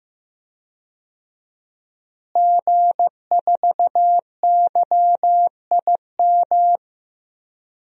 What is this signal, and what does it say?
Morse code, heard as a keyed tone, about 15 wpm, 700 Hz. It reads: G4YIM
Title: DXCC-100 Call Signs - 5 Characters - Repeated 3x - 15wpm